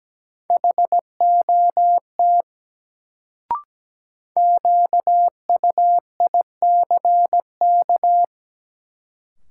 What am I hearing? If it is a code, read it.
Morse code, heard as a keyed tone, about 17 wpm, 700 Hz. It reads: HOT QUICK
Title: Sentences from Top 500 Words - New Word Drill - 17wpm